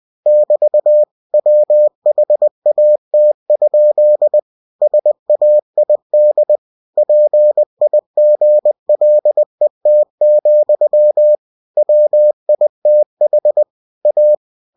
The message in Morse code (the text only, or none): <BT> WHAT? SAID PIGLET, WITH A